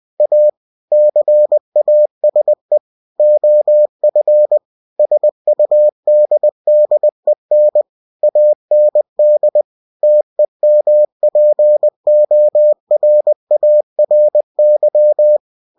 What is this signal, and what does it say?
Morse code, heard as a keyed tone, about 20 wpm, 600 Hz. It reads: A CASE OF SUDDEN AND TEMPORARY
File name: winnie-20wpm